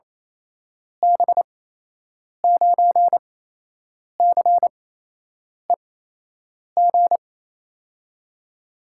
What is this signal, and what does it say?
Morse code, heard as a keyed tone, about 28 wpm, spaced as for 8 wpm, 700 Hz. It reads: B9CEG
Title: USA Call Signs - 8x Character Spacing - 28wpm